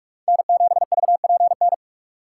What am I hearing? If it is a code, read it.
Morse code, heard as a keyed tone, about 46 wpm, 700 Hz. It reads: N7VPN
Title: USA Call Signs - 5 Characters - 3x Character Spacing - 45wpm